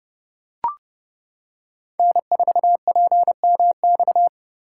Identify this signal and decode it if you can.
Morse code, heard as a keyed tone, about 28 wpm, 700 Hz. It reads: N4PMX